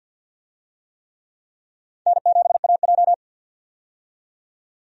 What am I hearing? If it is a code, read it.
Morse code, heard as a keyed tone, about 51 wpm, 700 Hz. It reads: N7AJ
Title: USA Call Signs - 4 Characters - 50wpm